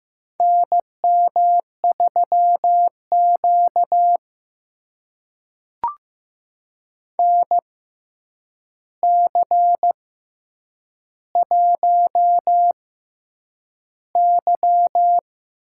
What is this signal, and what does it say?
Morse code, heard as a keyed tone, about 15 wpm, spaced as for 8 wpm, 700 Hz. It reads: NM3Q NC1Y
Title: USA Call Signs - 4 Characters - 6x Character Spacing - 15wpm